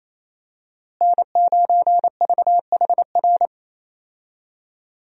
Morse code, heard as a keyed tone, about 28 wpm, 700 Hz. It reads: N9VHR
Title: USA Call Signs - 5 Characters - Repeated 3x - 28wpm